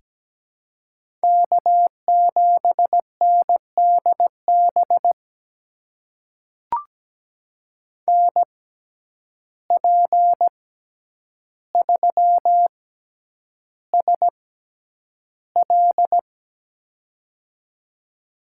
Morse code, heard as a keyed tone, about 17 wpm, spaced as for 9 wpm, 700 Hz. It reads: K7NDB NP3SL
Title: USA Call Signs - 5 Characters - 6x Character Spacing - 17wpm